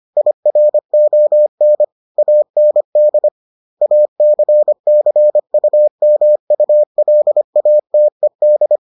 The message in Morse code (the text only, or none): IRON AND ACCUMULATED